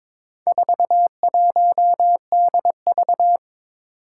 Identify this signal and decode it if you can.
Morse code, heard as a keyed tone, about 22 wpm, 700 Hz. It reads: 41DV
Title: Four Challenging Letter-Number Combinations 22wpm